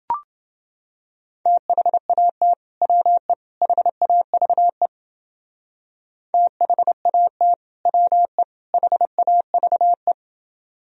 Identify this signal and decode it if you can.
Morse code, heard as a keyed tone, about 30 wpm, 700 Hz. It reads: THAT WE HAVE THAT WE HAVE